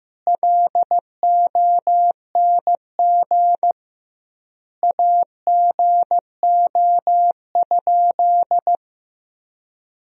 Morse code, heard as a keyed tone, about 15 wpm, 700 Hz. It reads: LONG AGO?
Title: 2x Word-Spacing - Sentences from Top 500 Words - 15wpm